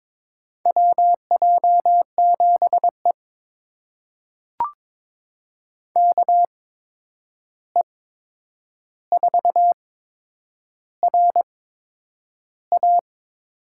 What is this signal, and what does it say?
Morse code, heard as a keyed tone, about 21 wpm, spaced as for 9 wpm, 700 Hz. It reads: WJ7E KE4RA
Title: USA Call Signs - 8x Character Spacing - 22wpm